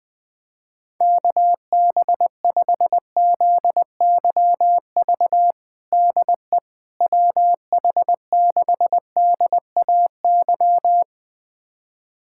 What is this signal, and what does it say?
Morse code, heard as a keyed tone, about 20 wpm, 700 Hz. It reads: KB5ZYV DE WH6DAY
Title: USA Callsign DE Callsign - 6 Characters - 20wpm